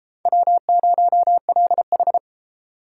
Morse code, heard as a keyed tone, about 32 wpm, 700 Hz. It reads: W0LH